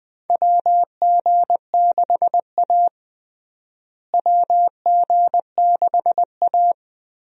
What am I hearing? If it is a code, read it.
Morse code, heard as a keyed tone, about 20 wpm, 700 Hz. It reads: WG6A WG6A
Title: USA Call Signs - 4 Characters - Repeated 2x - 20wpm